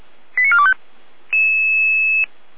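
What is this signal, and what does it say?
Five-tone selective call (ZVEI), heard as quick runs of five tones, tones of about 70 ms. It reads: 89425
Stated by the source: Five-tone ZVEI selective call received by SDR, code 88076.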